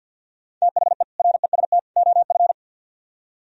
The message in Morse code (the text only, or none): THE REST OF